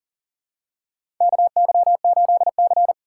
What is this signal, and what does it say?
Morse code, heard as a keyed tone, about 40 wpm, 700 Hz. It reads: KY8C
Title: USA Call Signs - 4 Characters - 3x Character Spacing - 40wpm